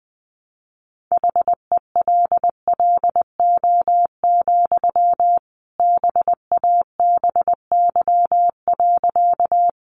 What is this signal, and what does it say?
Morse code, heard as a keyed tone, about 20 wpm, 700 Hz. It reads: HELLO, BABY.
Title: Sentences from Top 1300 Words 20wpm